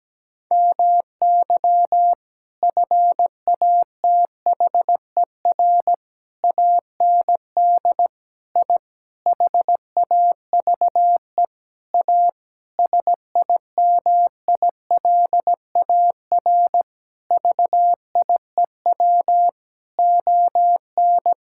Morse code, heard as a keyed tone, about 17 wpm, 700 Hz. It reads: MY FATHER AND I HAVE A SIMILAR VIEW ON
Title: Sentences from Top 1000 Words - Review Part 6 - 17wpm